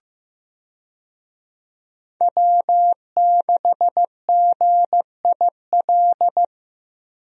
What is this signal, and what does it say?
Morse code, heard as a keyed tone, about 15 wpm, 700 Hz. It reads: W6GIL
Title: USA Call Signs - 5 Characters - Repeated 2x - 15wpm